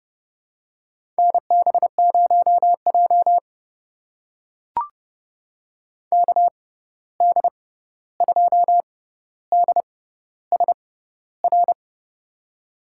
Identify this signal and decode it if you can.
Morse code, heard as a keyed tone, about 30 wpm, spaced as for 14 wpm, 700 Hz. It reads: NB0J KD2DSR
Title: USA Call Signs - 6x Character Spacing - 30wpm